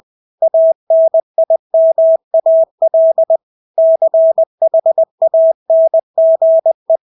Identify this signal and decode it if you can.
Morse code, heard as a keyed tone, about 20 wpm, 650 Hz. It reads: ANIMAL CHANGE
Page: Engelske ord 20 wpm